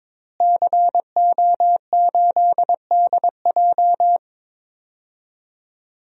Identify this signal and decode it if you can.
Morse code, heard as a keyed tone, about 22 wpm, 700 Hz. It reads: CO8DJ
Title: DXCC-100 Call Signs - 5 Characters - Repeated 3x - 22wpm